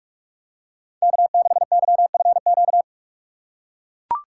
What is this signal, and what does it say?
Morse code, heard as a keyed tone, about 42 wpm, 700 Hz. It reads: K6YFQ